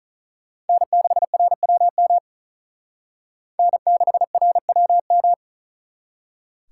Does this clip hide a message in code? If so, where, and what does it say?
Yes: Morse code from 0.7 to 5.3 s, heard as a keyed tone, about 44 wpm, 700 Hz. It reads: N6RWM N6RWM